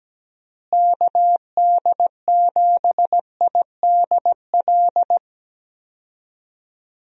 Morse code, heard as a keyed tone, about 17 wpm, 700 Hz. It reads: KD7IDL
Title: USA Call Signs - 6 Characters - Repeated 2x - 17wpm